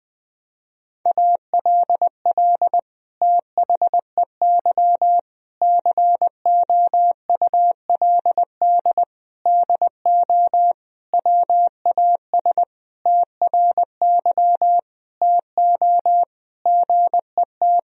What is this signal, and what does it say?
Morse code, heard as a keyed tone, about 20 wpm, 700 Hz. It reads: ALL THEY COULD DO WAS TRY TO GET
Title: Sentences from Top 1000 Words - Review Part 1 - 20wpm